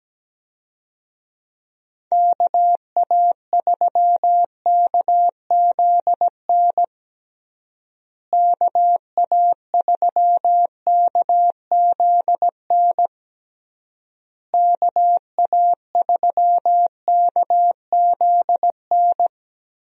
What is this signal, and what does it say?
Morse code, heard as a keyed tone, about 17 wpm, 700 Hz. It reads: KA3KZN KA3KZN KA3KZN
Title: USA Call Signs - 6 Characters - Repeated 3x - 17wpm